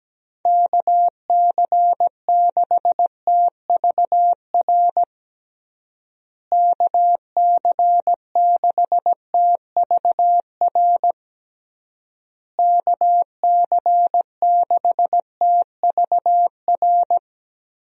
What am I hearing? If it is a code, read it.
Morse code, heard as a keyed tone, about 17 wpm, 700 Hz. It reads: KC6TVR KC6TVR KC6TVR
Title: USA Call Signs - 6 Characters - Repeated 3x - 17wpm